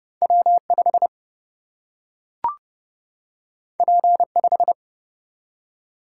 Morse code, heard as a keyed tone, about 29 wpm, 700 Hz. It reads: W5 P5
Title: Single-Letter Single-Number 30wpm